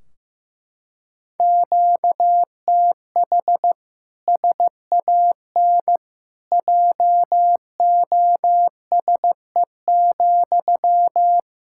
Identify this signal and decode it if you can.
Morse code, heard as a keyed tone, about 15 wpm, 700 Hz. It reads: QTH SAN JOSE,
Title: Practice QSO - Exchange 1 - Beginning - 15wpm